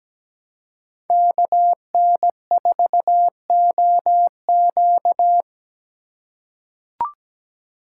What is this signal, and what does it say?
Morse code, heard as a keyed tone, about 17 wpm, 700 Hz. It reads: KN4OQ
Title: USA Call Signs - 5 Characters - 4x Character Spacing - 17wpm